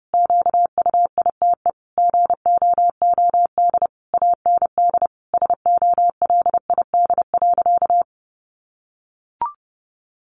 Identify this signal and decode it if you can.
Morse code, heard as a keyed tone, about 30 wpm, 700 Hz. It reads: QUITE GOOD AND SOLID.
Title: Sentences from Top 1500 Words 30wpm